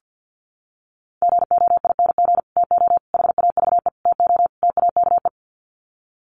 Morse code, heard as a keyed tone, about 50 wpm, 700 Hz. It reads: GOING TO HAVE TO TAKE
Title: Sets of 5 Words Encore 50wpm